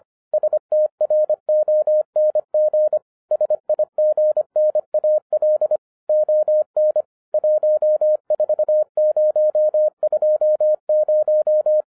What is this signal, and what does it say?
Morse code, heard as a keyed tone, about 25 wpm, 610 Hz. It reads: STRONG SIGNAL ON 14020
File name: 2012-01-mp3 / n06-FIRST FISTS CONTACT-25wpm